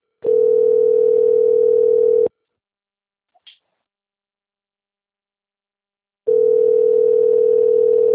Ringback tone (North America), 440 + 480 Hz.